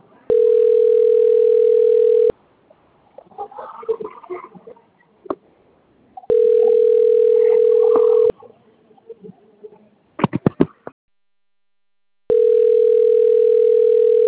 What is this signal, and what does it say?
Ringback tone (North America), 440 + 480 Hz.